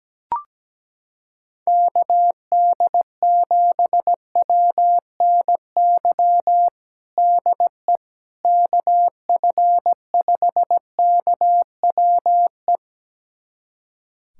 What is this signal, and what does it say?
Morse code, heard as a keyed tone, about 17 wpm, 700 Hz. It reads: KD7WNY DE KF5KWE
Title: USA Callsign DE Callsign - 6 Characters - 17wpm